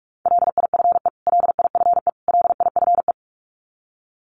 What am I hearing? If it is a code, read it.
Morse code, heard as a keyed tone, about 44 wpm, 700 Hz. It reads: LIFE LIFE LIFE
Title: Top 500 Words Repeated 45wpm